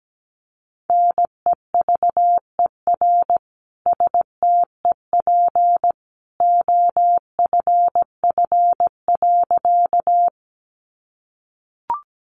Morse code, heard as a keyed tone, about 17 wpm, 700 Hz. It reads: NEVER STEP OFF.
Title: A Sound Of Thunder 17wpm